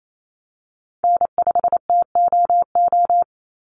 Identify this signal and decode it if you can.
Morse code, heard as a keyed tone, about 28 wpm, 700 Hz. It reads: N5TOO
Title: Call Signs - 28wpm